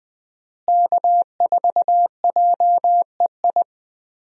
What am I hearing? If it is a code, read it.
Morse code, heard as a keyed tone, about 20 wpm, 700 Hz. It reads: K4JEI
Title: USA Call Signs - 5 Characters - Repeated 2x - 20wpm